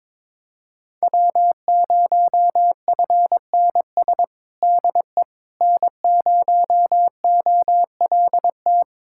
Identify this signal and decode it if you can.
Morse code, heard as a keyed tone, about 22 wpm, 700 Hz. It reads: W0FNS DE N0OLT
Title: USA Callsign DE Callsign - 5 Characters - 22wpm